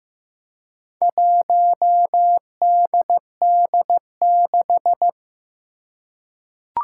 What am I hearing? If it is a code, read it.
Morse code, heard as a keyed tone, about 15 wpm, 700 Hz. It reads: 1DD6
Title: Four Challenging Letter-Number Combinations 15wpm